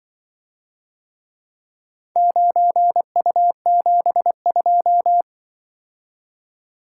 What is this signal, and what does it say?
Morse code, heard as a keyed tone, about 24 wpm, 700 Hz. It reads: 9U72
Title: Speed-Racing - Four Letter-Number Combinations - 20wpm